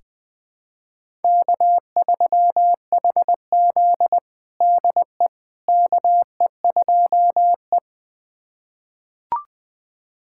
Morse code, heard as a keyed tone, about 20 wpm, 700 Hz. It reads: K3HZ DE KE2E
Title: USA Callsign DE Callsign - 4 Characters - 20wpm